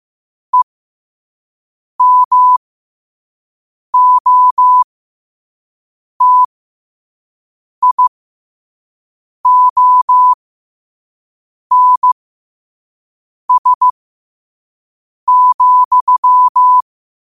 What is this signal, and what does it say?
Morse code, heard as a keyed tone, about 15 wpm, spaced as for 5 wpm, 1000 Hz. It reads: EMOTIONS,
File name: qotdfiles / 5wpm